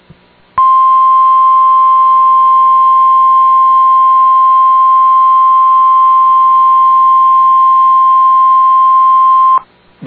Emergency Alert System (SAME) tones (the 1050 Hz attention tone). It is heard from 0.6 to 9.3 s.